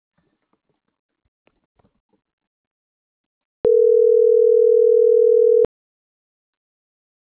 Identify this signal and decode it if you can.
Ringback tone (North America), 440 + 480 Hz.